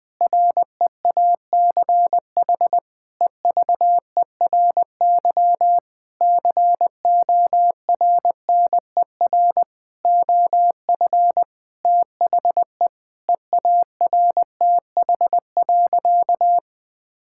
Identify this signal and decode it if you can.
Morse code, heard as a keyed tone, about 21 wpm, 700 Hz. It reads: REACH EVERY CORNER OF THE EARTH.
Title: Sentences from Top 1600 Words 20wpm